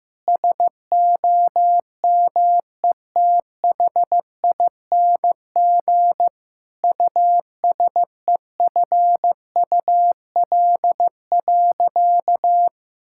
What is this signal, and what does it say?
Morse code, heard as a keyed tone, about 15 wpm, 700 Hz. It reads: SOMETHING USEFUL.
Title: Sentences from Top 1000 Words - Review Part 5 - 15wpm